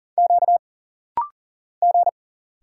Morse code, heard as a keyed tone, about 40 wpm, 700 Hz. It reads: Q G
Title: Single Letters - Rapid-Fire 40wpm